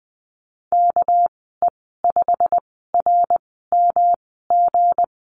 Morse code, heard as a keyed tone, about 20 wpm, spaced as for 15 wpm, 700 Hz. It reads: KE5RMG
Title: 200 Lesson 32g - Letter k - Call Signs 20wpm